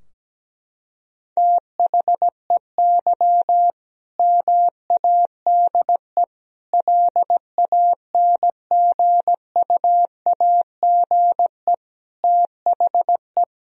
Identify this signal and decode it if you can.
Morse code, heard as a keyed tone, about 17 wpm, 700 Hz. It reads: THEY MADE LANGUAGE THE
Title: Sentences from Top 1000 Words - Review Part 3 - 17wpm